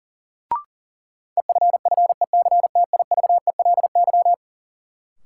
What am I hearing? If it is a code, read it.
Morse code, heard as a keyed tone, about 37 wpm, 700 Hz. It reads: EFFECTIVELY
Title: Sentences from Top 1400 Words - New Word Drill - 40wpm